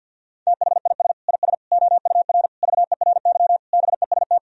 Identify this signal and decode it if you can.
Morse code, heard as a keyed tone, about 50 wpm, 700 Hz. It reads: THIS IS OUR VERY BEST
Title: Sentences from Top 1000 Words - Review Part 4 - 50wpm